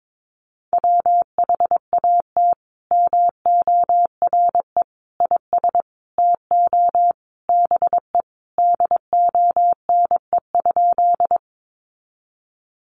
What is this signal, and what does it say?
Morse code, heard as a keyed tone, about 22 wpm, 700 Hz. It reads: WHAT MORE IS TO BE DONE?
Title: Sentences from Top 500 Words 22wpm